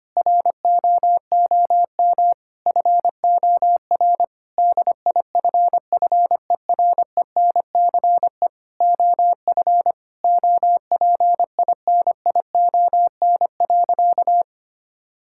Morse code, heard as a keyed tone, about 25 wpm, 700 Hz. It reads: ROOM FOR DIFFERENCE OF OPINION.
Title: Sentences from Top 1600 Words 25wpm